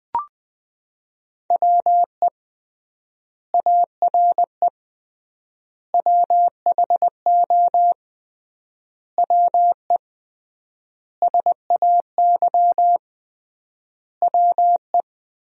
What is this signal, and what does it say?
Morse code, heard as a keyed tone, about 20 wpm, 700 Hz. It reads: WE ARE WHO WE SAY WE
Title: 3x Word-Spacing - Sentences from Top 200 Words - 20wpm